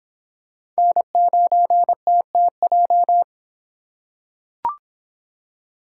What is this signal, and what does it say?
Morse code, heard as a keyed tone, about 26 wpm, 700 Hz. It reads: N9TTJ